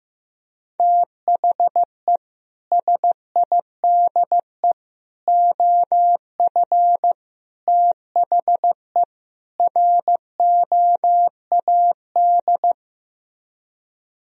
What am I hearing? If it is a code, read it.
Morse code, heard as a keyed tone, about 15 wpm, 700 Hz. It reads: THE SIDE OF THE ROAD
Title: Speed-Racing - Sets of 5 Words 15wpm